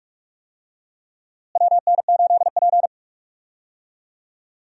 Morse code, heard as a keyed tone, about 46 wpm, 700 Hz. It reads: WN8P
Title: USA Call Signs - 4 Characters - Repeated 3x - 45wpm